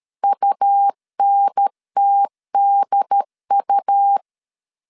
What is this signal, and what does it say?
Morse code, heard as a keyed tone, about 13 wpm, 800 Hz. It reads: UNTDU
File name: CW / DET / 1 3 1500C S C 12wpm 2750